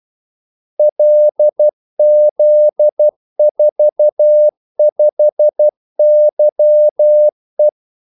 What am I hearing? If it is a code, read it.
Morse code, heard as a keyed tone, about 12 wpm, 600 Hz. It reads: LZ45YE
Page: 2 ekstra ordmellemrum mellem kaldesignalerne